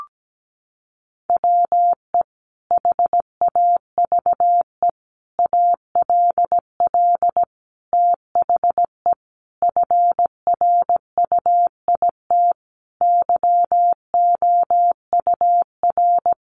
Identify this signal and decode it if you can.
Morse code, heard as a keyed tone, about 17 wpm, 700 Hz. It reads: WE HAVE ALL THE FRUIT YOUR
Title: Sentences from Top 1300 Words 17wpm